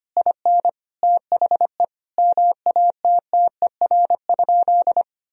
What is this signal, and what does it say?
Morse code, heard as a keyed tone, about 25 wpm, 700 Hz. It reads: IN THE MATTER?
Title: Sentences from Top 1000 Words - Review Part 1 - 25wpm